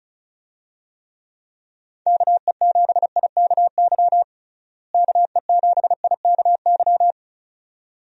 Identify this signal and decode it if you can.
Morse code, heard as a keyed tone, about 36 wpm, 700 Hz. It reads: KE7IKY KE7IKY
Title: USA Call Signs - 6 Characters - Repeated 2x - 35wpm